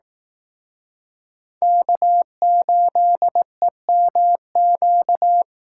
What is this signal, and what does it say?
Morse code, heard as a keyed tone, about 18 wpm, 700 Hz. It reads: K8EMQ
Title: Speed Racing - USA 1x3 Call Signs 15wpm